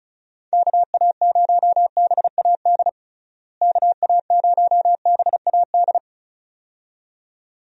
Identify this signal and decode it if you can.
Morse code, heard as a keyed tone, about 36 wpm, 700 Hz. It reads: KA0BAD KA0BAD
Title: USA Call Signs - 6 Characters - Repeated 2x - 35wpm